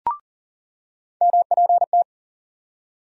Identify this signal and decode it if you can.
Morse code, heard as a keyed tone, about 40 wpm, 700 Hz. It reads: MPT
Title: USA Call Sign Postfixes - 3 Letters 40wpm